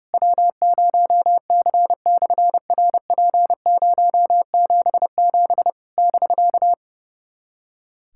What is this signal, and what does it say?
Morse code, heard as a keyed tone, about 30 wpm, 700 Hz. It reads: W0C/RP077 <BK>
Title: SOTA QSO Practice 30wpm